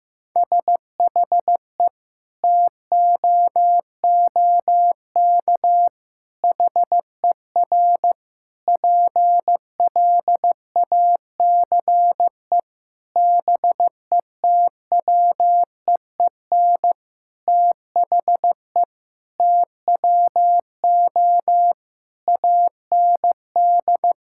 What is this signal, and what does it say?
Morse code, heard as a keyed tone, about 15 wpm, 700 Hz. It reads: SHE TOOK HER PLACE BETWEEN THE TWO AND
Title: Sentences from Top 1000 Words - Review Part 5 - 15wpm